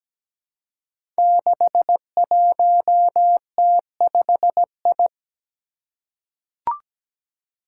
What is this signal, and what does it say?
Morse code, heard as a keyed tone, about 17 wpm, 700 Hz. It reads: 61T5I
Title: Five Challenging Letter-Number Combinations 17wpm